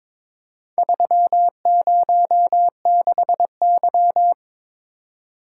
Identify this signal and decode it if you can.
Morse code, heard as a keyed tone, about 23 wpm, 700 Hz. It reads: 306Y
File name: practice / 22wpm / Four Letter-Number Combinations - 22wpm